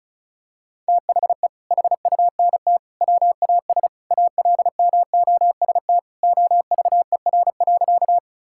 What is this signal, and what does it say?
Morse code, heard as a keyed tone, about 36 wpm, 700 Hz. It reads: THE HUNT WAS ALMOST OVER.
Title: Sentences from Top 1600 Words 35wpm